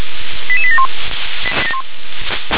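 Five-tone selective call (ZVEI), heard as quick runs of five tones, tones of about 70 ms. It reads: 88871
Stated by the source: ZVEI five-tone selective call received on an SDR.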